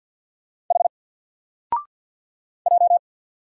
Morse code, heard as a keyed tone, about 50 wpm, 700 Hz. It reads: H J
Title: Single Letter-Number Alt 50wpm